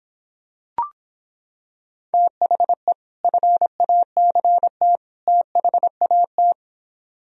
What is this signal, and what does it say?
Morse code, heard as a keyed tone, about 26 wpm, 700 Hz. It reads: THE FACT THAT